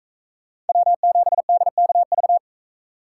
Morse code, heard as a keyed tone, about 42 wpm, 700 Hz. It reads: W7DKV